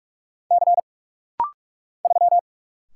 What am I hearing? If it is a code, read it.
Morse code, heard as a keyed tone, about 44 wpm, 700 Hz. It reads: C 3
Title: Single Letter-Number - Rapid-Fire 45wpm